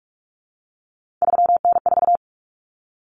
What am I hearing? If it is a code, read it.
Morse code, heard as a keyed tone, about 44 wpm, 700 Hz. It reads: 3N4
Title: Common Cut Numbers - 3-Digit 45wpm